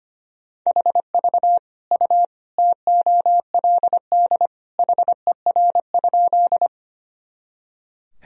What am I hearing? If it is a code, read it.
Morse code, heard as a keyed tone, about 25 wpm, 700 Hz. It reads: HV U TOLD HER?
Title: Sentences from Top 400 Words and Abbreviations 25wpm